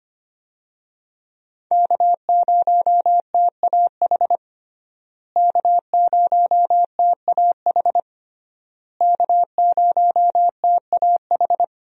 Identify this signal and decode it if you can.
Morse code, heard as a keyed tone, about 25 wpm, 700 Hz. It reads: K0TAH K0TAH K0TAH
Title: USA Call Signs - 5 Characters - Repeated 3x - 25wpm